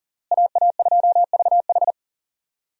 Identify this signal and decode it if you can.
Morse code, heard as a keyed tone, about 37 wpm, 700 Hz. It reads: AA2VH
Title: USA Call Signs - 5 Characters - Repeated 3x - 40wpm